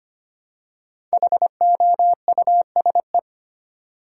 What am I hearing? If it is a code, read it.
Morse code, heard as a keyed tone, about 25 wpm, 700 Hz. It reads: HOUSE
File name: 5 Letter Words - First 3 Letters Given - 25wpm